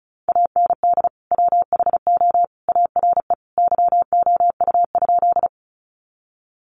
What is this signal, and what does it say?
Morse code, heard as a keyed tone, about 34 wpm, 700 Hz. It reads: AND WHO ARE YOU?
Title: Sentences from Top 100 Words 35wpm